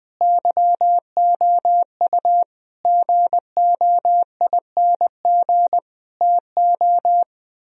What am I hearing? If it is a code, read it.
Morse code, heard as a keyed tone, about 20 wpm, 700 Hz. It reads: YOU GOING TO
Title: Speed-Racing - Sets of 5 Words 17wpm